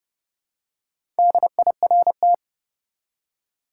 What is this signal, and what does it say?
Morse code, heard as a keyed tone, about 32 wpm, 700 Hz. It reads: DIRT
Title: Four Letter Words - 30wpm